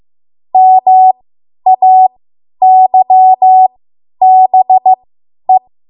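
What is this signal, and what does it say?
Morse code, heard as a keyed tone, about 15 wpm, spaced as for 10 wpm, 750 Hz. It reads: MAYBE
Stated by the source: Morse code practice run at 10 WPM from a code practice archive.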